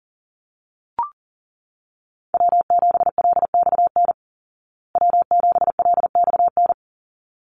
Morse code, heard as a keyed tone, about 40 wpm, 700 Hz. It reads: W7LXN W7LXN
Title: Call Signs Repeated 3x - 40wpm